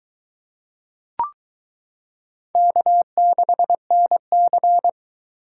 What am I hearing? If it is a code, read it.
Morse code, heard as a keyed tone, about 23 wpm, 700 Hz. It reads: K6NC